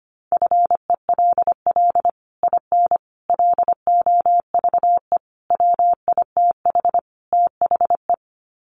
Morse code, heard as a keyed tone, about 25 wpm, 700 Hz. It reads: FELL IN LOVE WITH THE
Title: Sets of 5 Words Encore2 25wpm